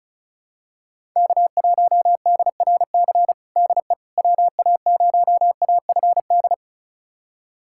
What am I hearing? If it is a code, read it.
Morse code, heard as a keyed tone, about 36 wpm, 700 Hz. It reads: K1DRC DE WA0AFD
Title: USA Callsign DE Callsign 35wpm